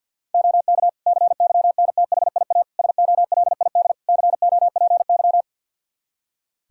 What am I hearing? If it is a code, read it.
Morse code, heard as a keyed tone, about 53 wpm, 700 Hz. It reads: OK CYNTHIA SOLID COPY
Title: K1USN SST QSO Practice 50wpm